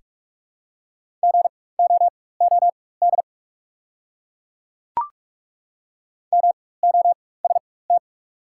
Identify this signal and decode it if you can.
Morse code, heard as a keyed tone, about 43 wpm, spaced as for 21 wpm, 700 Hz. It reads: GOOD MOST
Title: 4x Character Spacing - Top 200 Words - 45wpm